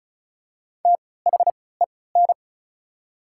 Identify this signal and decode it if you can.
Morse code, heard as a keyed tone, about 36 wpm, spaced as for 20 wpm, 690 Hz. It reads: THEN
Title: 3x Character Spacing - Top 500 Words - 35wpm